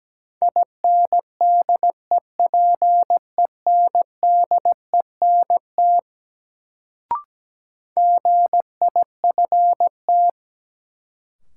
Morse code, heard as a keyed tone, about 17 wpm, 700 Hz. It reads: INDEPENDENT GIFT
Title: Sentences from Top 900 Words - New Word Drill - 17wpm